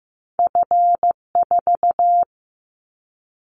Morse code, heard as a keyed tone, about 15 wpm, 700 Hz. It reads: F4
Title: Callsign Prefixes for Top 25 DXCC - 15wpm